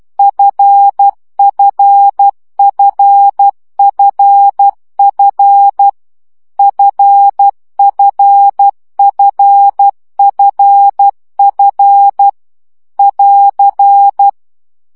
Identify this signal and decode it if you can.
Morse code, heard as a keyed tone, about 12 wpm, 800 Hz. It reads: FFFFF FFFFF <AR>
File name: CW gyakorlas f 10 baud